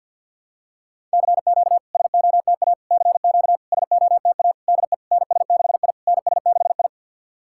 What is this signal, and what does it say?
Morse code, heard as a keyed tone, about 52 wpm, 690 Hz. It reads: CQ SOTA CQ SOTA DE NS6I NS6I
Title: SOTA QSO Practice 50wpm